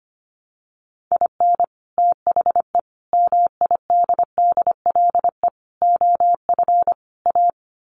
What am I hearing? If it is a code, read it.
Morse code, heard as a keyed tone, about 25 wpm, 700 Hz. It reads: IN THE MIDDLE OF A
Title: Sets of 5 Words 25wpm